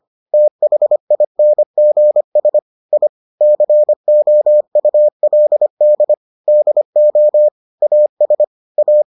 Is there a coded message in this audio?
Yes: Morse code from 0.3 to 9.0 s, heard as a keyed tone, about 25 wpm, 600 Hz. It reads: THINGS I COULD DO AS A